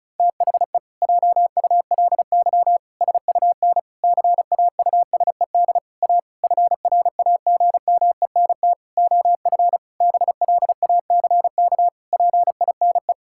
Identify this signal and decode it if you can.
Morse code, heard as a keyed tone, about 36 wpm, 700 Hz. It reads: THE JULY SUN CAUSED A FRAGMENT OF BLACK PINE
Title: Pangram Challenge v2 - 35wpm